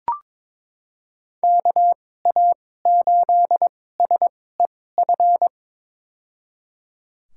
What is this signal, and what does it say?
Morse code, heard as a keyed tone, about 21 wpm, spaced as for 16 wpm, 700 Hz. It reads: KA8SEF
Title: USA Call Signs - 6 Characters - 2x Character Spacing - 22wpm